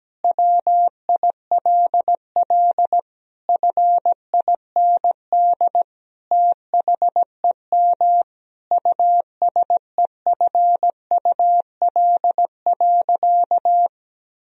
Morse code, heard as a keyed tone, about 17 wpm, 700 Hz. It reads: WILL FIND THEM USEFUL.
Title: Sentences from Top 1000 Words - Review Part 4 - 17wpm